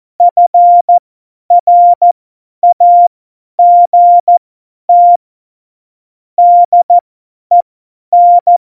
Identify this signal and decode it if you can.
Morse code, heard as a keyed tone, about 14 wpm, spaced as for 10 wpm, 700 Hz. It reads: FRAGT DEN
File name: Morsekurs / Witze / Witz 26. Februar 2019, 50 F70